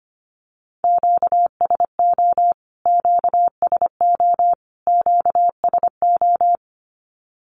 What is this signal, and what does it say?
Morse code, heard as a keyed tone, about 25 wpm, 700 Hz. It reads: QSO QSO QSO
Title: Q-Codes Repeated 3x 25wpm